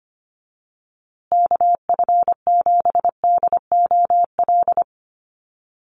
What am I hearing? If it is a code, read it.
Morse code, heard as a keyed tone, about 25 wpm, 700 Hz. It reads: KF7DOL
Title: USA Call Signs Repeated 2x 25wpm